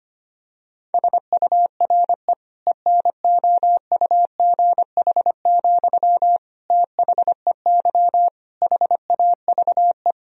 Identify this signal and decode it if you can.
Morse code, heard as a keyed tone, about 25 wpm, 700 Hz. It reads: SURE ENOUGH, THEY HAVE
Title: Sentences from Top 1000 Words - Review Part 2 - 25wpm